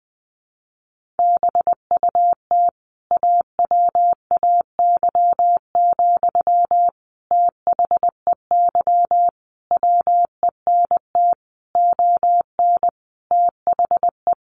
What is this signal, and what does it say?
Morse code, heard as a keyed tone, about 20 wpm, 700 Hz. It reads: BUT AWAY, THEY WENT ON THE
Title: Sentences from Top 1400 Words 20wpm